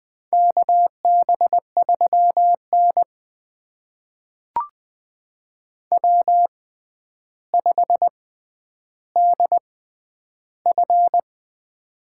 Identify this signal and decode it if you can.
Morse code, heard as a keyed tone, about 20 wpm, spaced as for 10 wpm, 700 Hz. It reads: KB3N W5DF
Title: USA Call Signs - 4 Characters - 6x Character Spacing - 20wpm